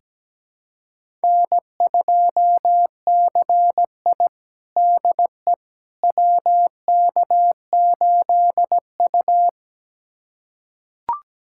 Morse code, heard as a keyed tone, about 17 wpm, 700 Hz. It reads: N2CI DE WK8U
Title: USA Callsign DE Callsign - 4 Characters - 17wpm